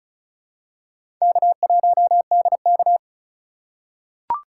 Morse code, heard as a keyed tone, about 34 wpm, 700 Hz. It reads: K1DK
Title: USA Call Signs - 4 Characters - 2x Character Spacing - 35wpm